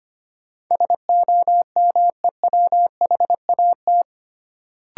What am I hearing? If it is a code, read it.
Morse code, heard as a keyed tone, about 25 wpm, 700 Hz. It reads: SOMEWHAT
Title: Sentences from Top 1300 Words - New Word Drill Alt - 25wpm